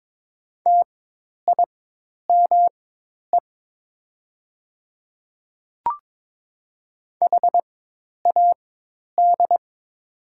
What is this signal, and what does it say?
Morse code, heard as a keyed tone, about 22 wpm, spaced as for 10 wpm, 700 Hz. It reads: TIME HAD
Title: 4x Character Spacing - Top 100 Words - 22wpm